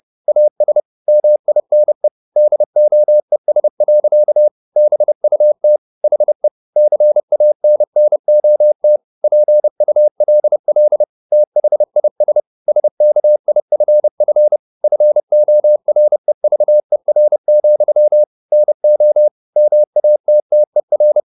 Morse code, heard as a keyed tone, about 30 wpm, 600 Hz. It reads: AS MINE DOES. BUT HE CANNOT PULL THIS SKIFF FOREVER, NO MATTER
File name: the old man and the sea / the old man and the sea 1 30wpm